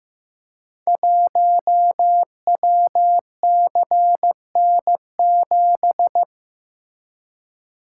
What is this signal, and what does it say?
Morse code, heard as a keyed tone, about 15 wpm, 700 Hz. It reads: 1WCN7
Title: Code Groups - 5 Letter:Numbers - 15wpm